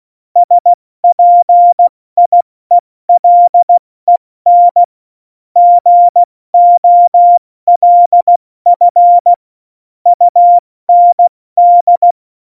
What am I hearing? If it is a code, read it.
Morse code, heard as a keyed tone, about 16 wpm, 700 Hz. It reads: SPIELEN GOLF UND
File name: Morsekurs / Witze / Witz 01. Dezember 2020, 70 F80